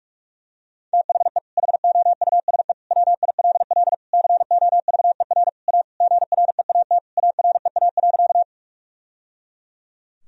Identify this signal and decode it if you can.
Morse code, heard as a keyed tone, about 41 wpm, 700 Hz. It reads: THE HOUSE WILL COVER A GREAT AREA.
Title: Sentences from Top 1000 Words - Review Part 2 - 45wpm